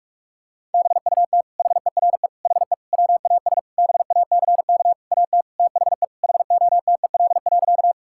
Morse code, heard as a keyed tone, about 46 wpm, 700 Hz. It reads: BUT HERE HE WAS BACK AT THE HOTEL.
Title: Sentences from Top 1600 Words 45wpm